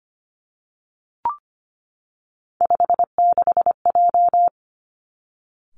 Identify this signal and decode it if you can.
Morse code, heard as a keyed tone, about 25 wpm, 700 Hz. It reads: 56J
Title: Three Challenging Letter-Number Combinations 25wpm